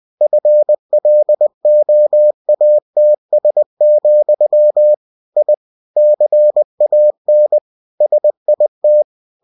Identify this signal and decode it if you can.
Morse code, heard as a keyed tone, about 20 wpm, 600 Hz. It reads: FLOATS, I CAN SIT
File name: cw-books / winnie-the-pooh / winnie-20wpm-page-0095